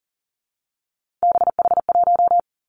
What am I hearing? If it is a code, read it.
Morse code, heard as a keyed tone, about 42 wpm, 700 Hz. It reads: BH1
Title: Three Challenging Letter-Number Combinations 40wpm